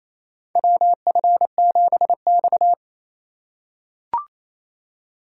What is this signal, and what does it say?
Morse code, heard as a keyed tone, about 28 wpm, 700 Hz. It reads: WF7X
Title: USA Call Signs - 3x Character Spacing - 28wpm